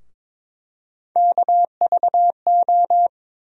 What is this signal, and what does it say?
Morse code, heard as a keyed tone, about 22 wpm, 700 Hz. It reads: KVO
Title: USA Call Sign Postfixes - 3 Letters 22wpm